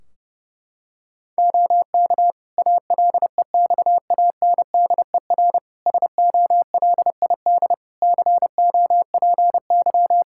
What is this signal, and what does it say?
Morse code, heard as a keyed tone, about 30 wpm, 700 Hz. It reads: OK ALEXANDER SOLID COPY
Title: K1USN SST QSO Practice 30wpm